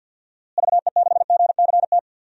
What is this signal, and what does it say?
Morse code, heard as a keyed tone, about 53 wpm, 700 Hz. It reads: VE6GCT
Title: DXCC-100 Call Signs - 6 Characters - Repeated 2x - 50wpm